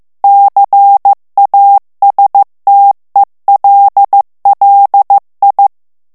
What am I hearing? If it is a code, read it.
Morse code, heard as a keyed tone, about 15 wpm, 800 Hz. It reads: CASTELLI
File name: ITALIEN texte clair 1 vitesse 14 mots minute